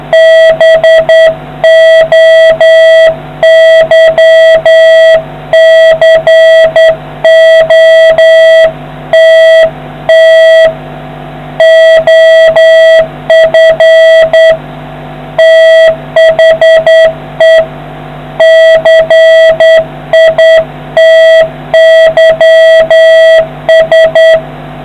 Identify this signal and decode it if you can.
Morse code, heard as a keyed tone, about 10 wpm, 650 Hz, with noise. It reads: BOYCOTT OF THE CITYS